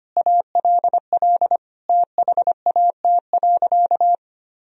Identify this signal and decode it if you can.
Morse code, heard as a keyed tone, about 25 wpm, 700 Hz. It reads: ALL THAT.
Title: Sentences from Top 1000 Words - Review Part 1 - 25wpm